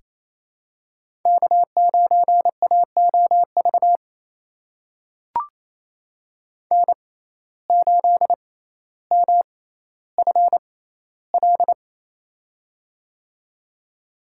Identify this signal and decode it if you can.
Morse code, heard as a keyed tone, about 28 wpm, spaced as for 14 wpm, 700 Hz. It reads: K9AOV N8MFL
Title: USA Call Signs - 5 Characters - 6x Character Spacing - 28wpm